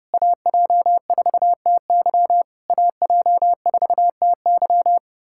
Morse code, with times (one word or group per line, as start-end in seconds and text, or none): 0.1-2.4 AJ4TY
2.7-5.0 AJ4TY